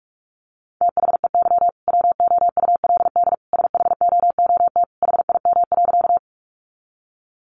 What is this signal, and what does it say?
Morse code, heard as a keyed tone, about 44 wpm, 700 Hz. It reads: THEY WOULD SHOOT HIM.
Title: Sentences from Top 1200 Words 45wpm